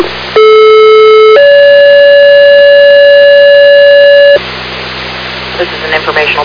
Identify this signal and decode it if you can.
Two-tone page, heard as a one-second tone, then a three-second tone, about 430 then 570 Hz, with speech. It is heard from 0.4 to 4.4 s.